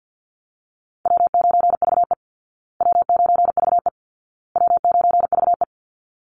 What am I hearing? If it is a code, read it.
Morse code, heard as a keyed tone, about 48 wpm, 700 Hz. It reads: W9VE W9VE W9VE
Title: Easy Call Signs Repeated 3x - 50wpm